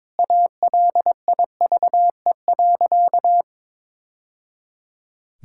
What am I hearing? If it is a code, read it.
Morse code, heard as a keyed tone, about 22 wpm, 700 Hz. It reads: ALIVE.
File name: practice / 22wpm / Sentences with 2-Word Phrases - 22wpm